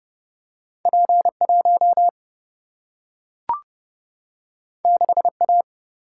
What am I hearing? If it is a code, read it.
Morse code, heard as a keyed tone, about 30 wpm, 700 Hz. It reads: P1 6A
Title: Single-Letter Single-Number 30wpm